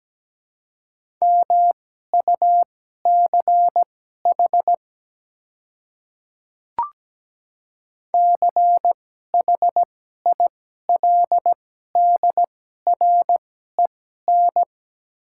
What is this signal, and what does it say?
Morse code, heard as a keyed tone, about 17 wpm, spaced as for 12 wpm, 690 Hz. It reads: MUCH CHILDREN
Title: 2x Character Spacing - Top 300 Words - 17wpm